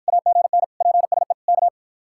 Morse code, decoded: AGN PSE K

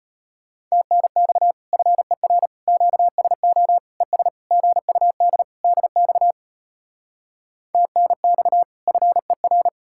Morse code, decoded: TNX FER QSO ES GUD DX TNX FER

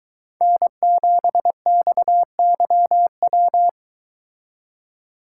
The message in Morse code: N7XYW

